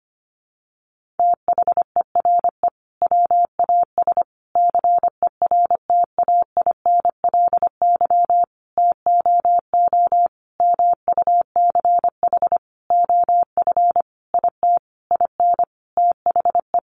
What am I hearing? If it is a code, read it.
Morse code, heard as a keyed tone, about 25 wpm, 700 Hz. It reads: THERE WAS CERTAINLY TOO MUCH OF IT IN THE
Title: Sentences from Top 900 Words 25wpm